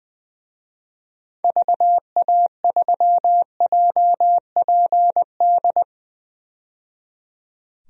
Morse code, heard as a keyed tone, about 20 wpm, 700 Hz. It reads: VA3JPD
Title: DXCC-100 Call Signs - 6 Characters - 20wpm